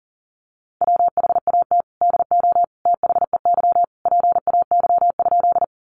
Morse code, heard as a keyed tone, about 40 wpm, 700 Hz. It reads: WHAT DO THEY PAY?
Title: Sentences from Top 700 Words 40wpm